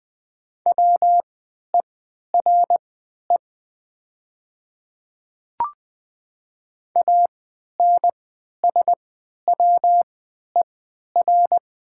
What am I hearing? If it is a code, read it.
Morse code, heard as a keyed tone, about 20 wpm, spaced as for 11 wpm, 700 Hz. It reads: WERE ANSWER
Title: 3x Character Spacing - Top 300 Words - 20wpm